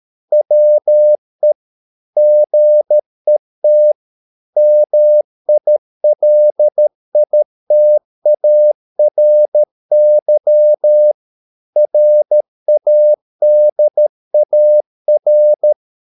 Morse code, heard as a keyed tone, about 13 wpm, 600 Hz. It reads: WE GET MILITARY RADAR